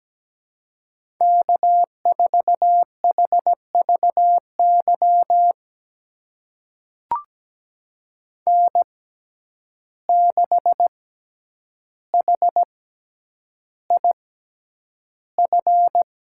Morse code, heard as a keyed tone, about 17 wpm, spaced as for 9 wpm, 700 Hz. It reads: K4HVY N6HIF
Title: USA Call Signs - 5 Characters - 6x Character Spacing - 17wpm